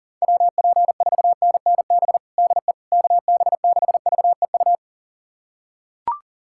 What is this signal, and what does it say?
Morse code, heard as a keyed tone, about 37 wpm, 700 Hz. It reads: WP4NNB DE KB6VEU
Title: USA Callsign DE Callsign - 6 Characters - 40wpm